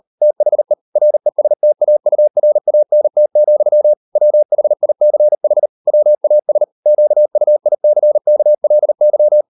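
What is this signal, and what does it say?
Morse code, heard as a keyed tone, about 38 wpm, 600 Hz. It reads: THE RESTAURANT, WHICH WAS QUICKLY
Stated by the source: Morse code practice at 38 WPM, sending Ken Follett's novel Never in English, from CW med Gnister.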